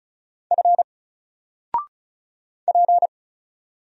Morse code, heard as a keyed tone, about 36 wpm, 700 Hz. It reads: F P
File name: Single-Letter-Number-35wpm